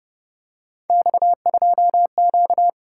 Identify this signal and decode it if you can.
Morse code, heard as a keyed tone, about 31 wpm, 700 Hz. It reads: X2Q